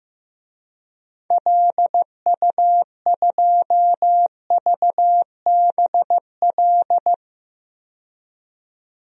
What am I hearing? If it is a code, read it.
Morse code, heard as a keyed tone, about 15 wpm, 700 Hz. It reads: LU2VBL
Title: DXCC-100 Call Signs - 6 Characters - 15wpm